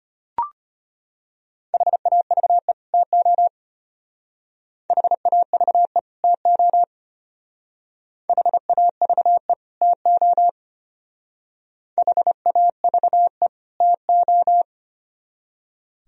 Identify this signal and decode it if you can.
Morse code, heard as a keyed tone, about 32 wpm, 700 Hz. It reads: HAVE TO HAVE TO HAVE TO HAVE TO